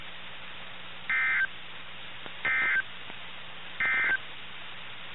Emergency Alert System (SAME) tones, heard as harsh buzzing bursts. It is heard from 1.0 to 4.3 s.
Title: NOAA Weather Radio - KIH61